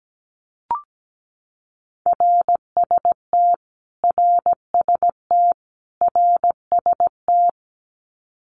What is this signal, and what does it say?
Morse code, heard as a keyed tone, about 17 wpm, 700 Hz. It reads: RST RST RST
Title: Abbreviations Repeated 3x - 17wpm